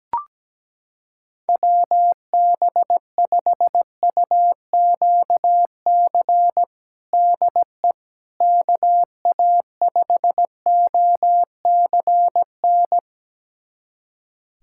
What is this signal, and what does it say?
Morse code, heard as a keyed tone, about 17 wpm, 700 Hz. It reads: WB5UQC DE KA5OCN
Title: USA Callsign DE Callsign - 6 Characters - 17wpm